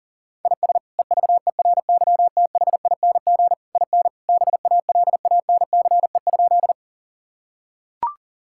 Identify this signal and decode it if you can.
Morse code, heard as a keyed tone, about 37 wpm, 700 Hz. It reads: IS EVERYTHING IN BALANCE?
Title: Sentences from Top 1000 Words - Review Part 6 - 40wpm